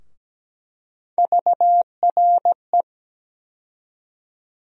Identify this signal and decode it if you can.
Morse code, heard as a keyed tone, about 17 wpm, 700 Hz. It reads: VRE
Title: USA Call Sign Postfixes - 3 Letters 17wpm